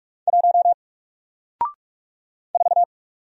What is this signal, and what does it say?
Morse code, heard as a keyed tone, about 44 wpm, 700 Hz. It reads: 1 4
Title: Single Letter-Number Alt 45wpm